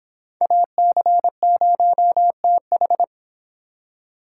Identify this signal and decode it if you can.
Morse code, heard as a keyed tone, about 30 wpm, 700 Hz. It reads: AC0TH